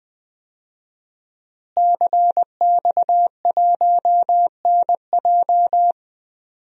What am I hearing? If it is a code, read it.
Morse code, heard as a keyed tone, about 20 wpm, 700 Hz. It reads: CX1NJ
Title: DXCC-100 Call Signs - 5 Characters - Repeated 2x - 20wpm